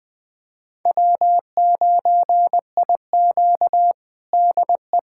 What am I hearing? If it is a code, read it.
Morse code, heard as a keyed tone, about 20 wpm, 700 Hz. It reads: W9IQ DE
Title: USA Callsign DE Callsign - 4 Characters - 20wpm